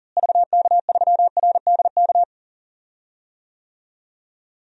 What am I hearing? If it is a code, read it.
Morse code, heard as a keyed tone, about 40 wpm, 700 Hz. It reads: VK3RDK
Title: DXCC-100 Call Signs - 6 Characters - Repeated 2x - 40wpm